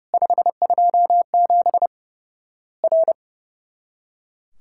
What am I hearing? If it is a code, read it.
Morse code, heard as a keyed tone, about 32 wpm, 700 Hz. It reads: 527 R
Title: Worked All Europe DX Contest - QTC Exchange - 30wpm